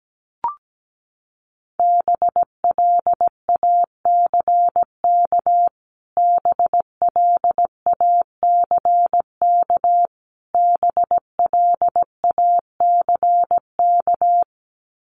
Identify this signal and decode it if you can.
Morse code, heard as a keyed tone, about 17 wpm, 700 Hz. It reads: BLACK BLACK BLACK
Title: Top 500 Words Repeated 17wpm